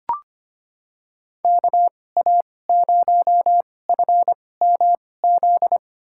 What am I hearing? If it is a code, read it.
Morse code, heard as a keyed tone, about 25 wpm, spaced as for 18 wpm, 700 Hz. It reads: KA0FMZ
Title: USA Call Signs - 6 Characters - 2x Character Spacing - 25wpm